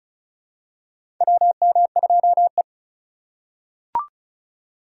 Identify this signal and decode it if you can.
Morse code, heard as a keyed tone, about 36 wpm, 700 Hz. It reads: WM2E
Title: USA Call Signs - 4 Characters - 2x Character Spacing - 35wpm